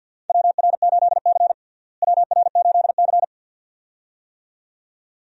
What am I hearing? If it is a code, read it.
Morse code, heard as a keyed tone, about 50 wpm, 690 Hz. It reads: WR8C WR8C
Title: USA Call Signs - 4 Characters - Repeated 2x - 50wpm